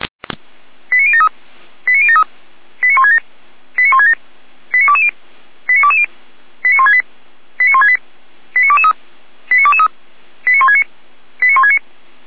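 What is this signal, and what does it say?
Five-tone selective call (ZVEI), heard as quick runs of five tones, tones of about 70 ms. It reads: 89073 89073 89167 89167 89220 89220 89168 89168 89223 89223 89169 89169
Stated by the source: ZVEI five-tone call 89060 received on SDR.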